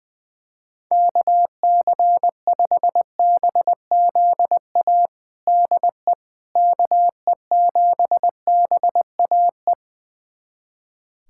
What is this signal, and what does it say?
Morse code, heard as a keyed tone, about 21 wpm, 700 Hz. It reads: KC5BZA DE KE7BAE
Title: USA Callsign DE Callsign 20wpm